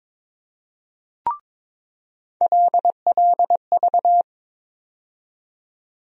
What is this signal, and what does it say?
Morse code, heard as a keyed tone, about 22 wpm, 700 Hz. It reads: LLV
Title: USA Call Sign Postfixes - 3 Letters 22wpm